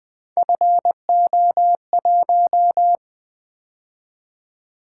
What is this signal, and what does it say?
Morse code, heard as a keyed tone, about 20 wpm, 700 Hz. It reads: FO1
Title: Three Letter-Number Combinations - 20wpm